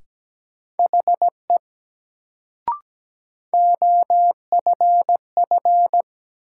Morse code, heard as a keyed tone, about 17 wpm, 700 Hz. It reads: HE OFF
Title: Sets of 2 Words - New Word Drill - 17wpm